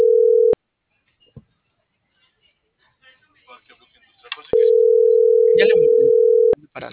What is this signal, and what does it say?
Ringback tone (North America), 440 + 480 Hz, with speech.
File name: CENTRAL SAGOT